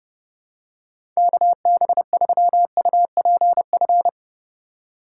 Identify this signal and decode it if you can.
Morse code, heard as a keyed tone, about 30 wpm, 700 Hz. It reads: KB3UPF
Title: USA Call Signs - 6 Characters - 3x Character Spacing - 30wpm